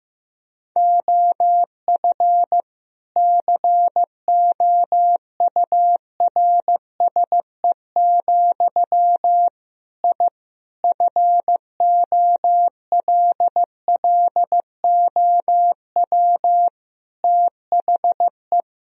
Morse code, heard as a keyed tone, about 15 wpm, 700 Hz. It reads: OF COURSE, I FOLLOW THE
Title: Sentences from Top 1000 Words - Review Part 4 - 15wpm